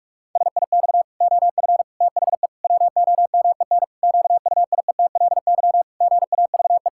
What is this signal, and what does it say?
Morse code, heard as a keyed tone, about 46 wpm, 700 Hz. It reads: SIX OF THE WOMEN QUIETLY GAVE
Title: Pangram Challenge v2 - 45wpm